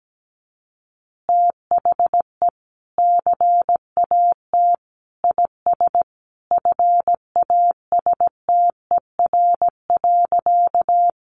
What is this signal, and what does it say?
Morse code, heard as a keyed tone, about 17 wpm, 700 Hz. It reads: THE CAT IS FASTER.
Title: Additional Easy Phrases 17wpm